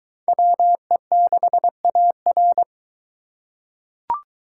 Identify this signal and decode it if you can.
Morse code, heard as a keyed tone, about 20 wpm, 700 Hz. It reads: WE6AR